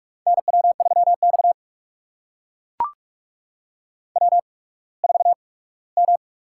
Morse code, heard as a keyed tone, about 44 wpm, 700 Hz. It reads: N W 3 X W 4 M
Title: USA Call Signs - 4 Characters - 8x Character Spacing - 45wpm